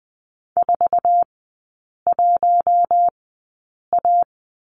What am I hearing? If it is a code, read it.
Morse code, heard as a keyed tone, about 20 wpm, spaced as for 8 wpm, 700 Hz. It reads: 41A
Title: 107 Lesson 18e - Letter m - Three Characters 20wpm